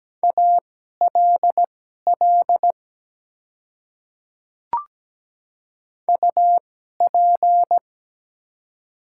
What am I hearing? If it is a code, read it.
Morse code, heard as a keyed tone, about 17 wpm, spaced as for 12 wpm, 690 Hz. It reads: ALL UP
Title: 2x Character Spacing - Top 100 Words - 17wpm